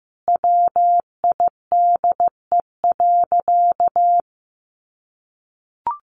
Morse code, heard as a keyed tone, about 15 wpm, 700 Hz. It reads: WIDE.
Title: Sentences from Top 1200 Words 15wpm